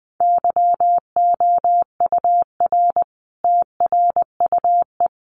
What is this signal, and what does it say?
Morse code, heard as a keyed tone, about 20 wpm, 700 Hz. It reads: YOUR TRUE
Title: Sentences from Top 400 Words 20wpm